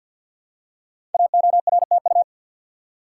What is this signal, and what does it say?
Morse code, heard as a keyed tone, about 46 wpm, 700 Hz. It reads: AORTU
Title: Code Groups - 5 Letters - 50wpm